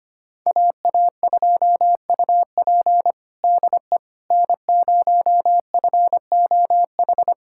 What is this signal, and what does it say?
Morse code, heard as a keyed tone, about 25 wpm, 700 Hz. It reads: AA2UP DE N0FOH
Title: USA Callsign DE Callsign - 5 Characters - 25wpm